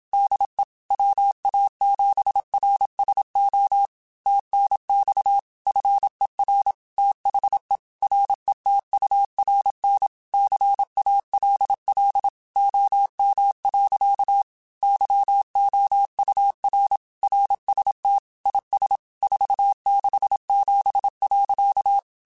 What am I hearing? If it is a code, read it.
Morse code, heard as a keyed tone, about 27 wpm, 800 Hz. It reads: DE WA7RSO TNX FER THE RETURN CALL OM. YOUR RST IS 467.